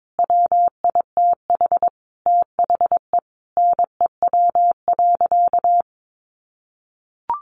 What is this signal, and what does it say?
Morse code, heard as a keyed tone, about 22 wpm, 700 Hz. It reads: WITH THE NEW.
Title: Sentences from Top 1300 Words 22wpm